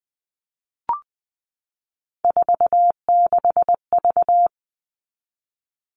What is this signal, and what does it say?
Morse code, heard as a keyed tone, about 20 wpm, 700 Hz. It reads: 46V
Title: All Cut Numbers - 3-Digit 20wpm